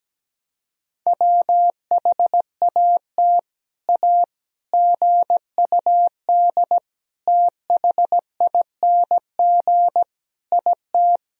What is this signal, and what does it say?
Morse code, heard as a keyed tone, about 17 wpm, 700 Hz. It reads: WHAT A GUD THING IT
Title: Sentences from Top 200 Words and Abbreviations 17wpm